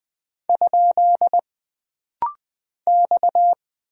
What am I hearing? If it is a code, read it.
Morse code, heard as a keyed tone, about 20 wpm, 700 Hz. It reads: ? X
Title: Single Character (Letter:Number:Punctuation:Prosign) - Rapid-Fire - 20wpm